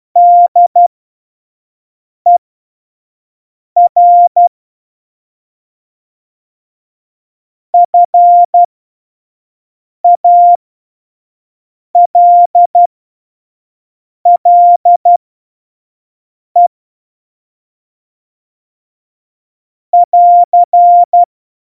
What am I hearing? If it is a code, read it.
Morse code, heard as a keyed tone, about 12 wpm, spaced as for 5 wpm, 700 Hz. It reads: DER FALLE <AR>
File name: Morsekurs / Witze / Witz 03 April 2018, 25 F70